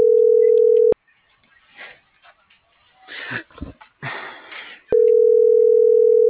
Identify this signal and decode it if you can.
Ringback tone (North America), 440 + 480 Hz.